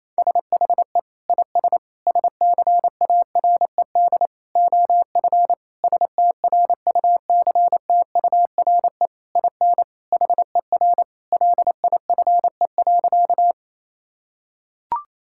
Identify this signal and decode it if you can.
Morse code, heard as a keyed tone, about 28 wpm, 700 Hz. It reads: SHE IS SCARED OF STRUCTURE IN HER LIFE.
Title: Sentences from Top 1000 Words - Review Part 6 - 28wpm